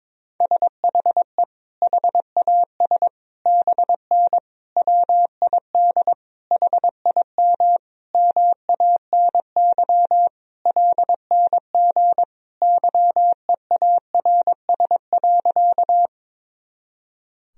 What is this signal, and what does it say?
Morse code, heard as a keyed tone, about 22 wpm, 700 Hz. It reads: SHE HAS BN WID HIM MANY LNG YEARS.
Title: Sentences from Top 200 Words and Abbreviations 22wpm